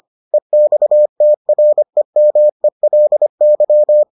Morse code, heard as a keyed tone, about 25 wpm, 600 Hz. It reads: EXTREMELY